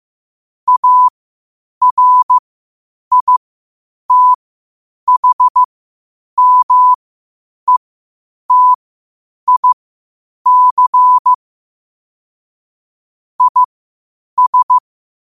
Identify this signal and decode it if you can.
Morse code, heard as a keyed tone, about 15 wpm, spaced as for 8 wpm, 1000 Hz. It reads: ARITHMETIC IS
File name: qotdfiles / 8wpm